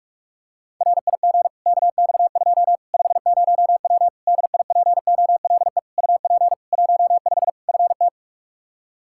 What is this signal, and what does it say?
Morse code, heard as a keyed tone, about 43 wpm, 700 Hz. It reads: RIG KX2 50W DIPOLE UP 15 FT
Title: K1USN SST QSO Practice 45wpm